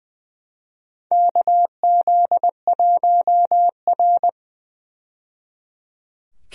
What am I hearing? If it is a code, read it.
Morse code, heard as a keyed tone, about 20 wpm, 700 Hz. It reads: KZ1R
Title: USA Call Signs - 4 Characters - 20wpm